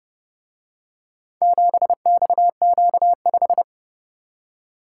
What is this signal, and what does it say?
Morse code, heard as a keyed tone, about 30 wpm, 700 Hz. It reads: 7XQ5